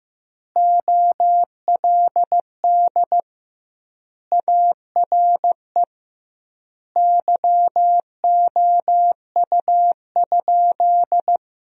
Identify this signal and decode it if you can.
Morse code, heard as a keyed tone, about 16 wpm, 700 Hz. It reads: OLD ARE YOU?
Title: 2x Word-Spacing - Sentences from Top 200 Words - 15wpm